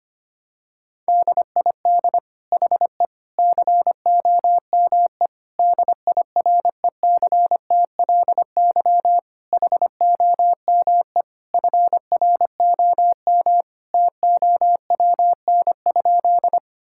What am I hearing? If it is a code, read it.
Morse code, heard as a keyed tone, about 25 wpm, 700 Hz. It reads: DID HE COME DIRECTLY HOME FROM TOWN?
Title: Sentences from Top 1000 Words - Review Part 5 - 25wpm